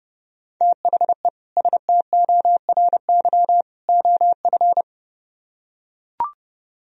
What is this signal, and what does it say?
Morse code, heard as a keyed tone, about 31 wpm, 700 Hz. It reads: THE STORY OF